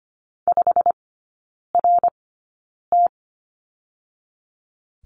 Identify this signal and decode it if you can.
Morse code, heard as a keyed tone, about 25 wpm, spaced as for 9 wpm, 690 Hz. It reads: 5RT
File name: Lesson 15e - Number 5 - Three Characters 25wpm